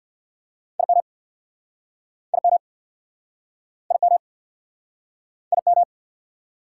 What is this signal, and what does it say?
Morse code, heard as a keyed tone, about 59 wpm, 690 Hz. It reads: IM IM IM IM
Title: Speed-Racing Prefixes and Suffixes 50wpm